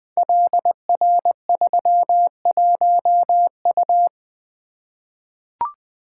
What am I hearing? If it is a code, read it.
Morse code, heard as a keyed tone, about 20 wpm, 700 Hz. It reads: LR31U
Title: Code Groups - 5 Letter:Numbers - 20wpm